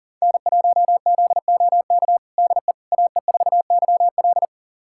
Morse code, heard as a keyed tone, about 40 wpm, 700 Hz. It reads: N1ZOK DE AE4YL
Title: USA Callsign DE Callsign - 5 Characters - 40wpm